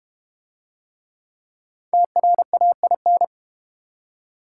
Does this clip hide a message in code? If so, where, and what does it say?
Yes: Morse code from 1.9 to 3.2 s, heard as a keyed tone, about 32 wpm, 690 Hz. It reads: TRAIN